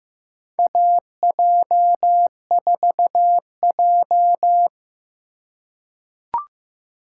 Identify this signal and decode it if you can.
Morse code, heard as a keyed tone, about 15 wpm, 700 Hz. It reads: AJ4J
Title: USA Call Signs - 4 Characters - 4x Character Spacing - 15wpm